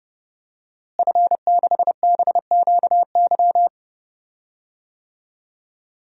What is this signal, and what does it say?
Morse code, heard as a keyed tone, about 32 wpm, 700 Hz. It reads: F6BQY
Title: DXCC-100 Call Signs - 5 Characters - Repeated 3x - 30wpm